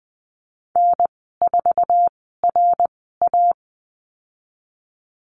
Morse code, heard as a keyed tone, about 20 wpm, spaced as for 15 wpm, 700 Hz. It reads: N4RA
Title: 055 Lesson 10g - Letter r - Call Signs 20wpm